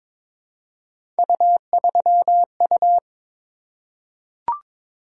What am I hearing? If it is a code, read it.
Morse code, heard as a keyed tone, about 22 wpm, 700 Hz. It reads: U3U
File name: practice / 22wpm / Three Letter-Number Combinations - 22wpm